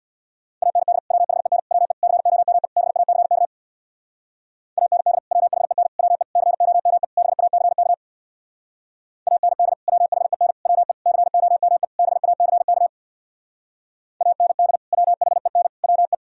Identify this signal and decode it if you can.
Morse code, heard as a keyed tone, about 60 wpm, 700 Hz. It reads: AND WHEN WE COME BACK AND WHEN WE COME BACK AND WHEN WE COME BACK AND WHEN WE
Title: Speed-Racing - Sets of 5 Words 50wpm